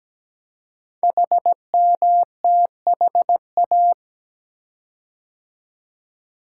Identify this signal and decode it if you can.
Morse code, heard as a keyed tone, about 17 wpm, 700 Hz. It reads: HMTHA
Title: Code Groups - 5 Letters - 17wpm